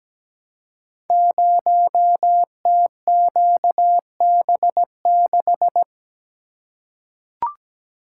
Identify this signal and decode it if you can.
Morse code, heard as a keyed tone, about 17 wpm, 700 Hz. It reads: 0TQB6
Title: Code Groups - 5 Letter:Numbers - 17wpm